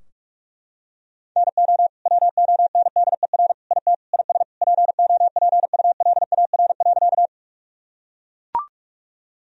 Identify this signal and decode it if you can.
Morse code, heard as a keyed tone, about 44 wpm, 700 Hz. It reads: NO WONDER IT IS POPULAR.
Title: Sentences from Top 1000 Words - Review Part 3 - 45wpm